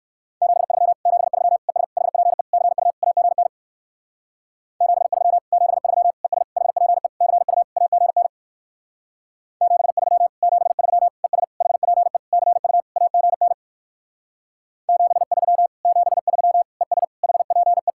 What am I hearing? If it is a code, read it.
Morse code, heard as a keyed tone, about 59 wpm, 700 Hz. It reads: 73 73 ES HPE CU AGN 73 73 ES HPE CU AGN 73 73 ES HPE CU AGN 73 73 ES HPE